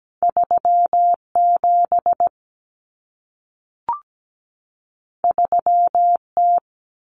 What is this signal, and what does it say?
Morse code, heard as a keyed tone, about 17 wpm, 700 Hz. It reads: 37 3T
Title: Common Cut Numbers - 2-Digit 17wpm